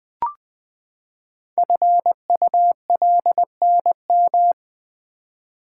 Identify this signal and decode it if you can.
Morse code, heard as a keyed tone, about 20 wpm, 700 Hz. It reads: FULNM
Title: Code Groups - 5 Letters - 20wpm